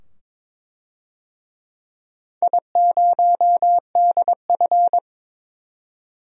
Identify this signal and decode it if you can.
Morse code, heard as a keyed tone, about 22 wpm, 700 Hz. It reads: I0DF
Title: DXCC-100 Call Signs - 4 Characters - 22wpm